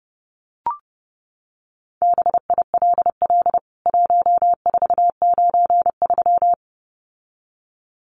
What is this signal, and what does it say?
Morse code, heard as a keyed tone, about 30 wpm, 700 Hz. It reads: BILL 1493
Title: CWT Exchange 30wpm